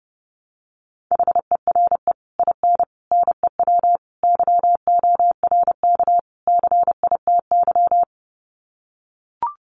Morse code, heard as a keyed tone, about 30 wpm, 700 Hz. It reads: HERE IN NEW YORK CITY
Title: Sets of 5 Words Encore2 30wpm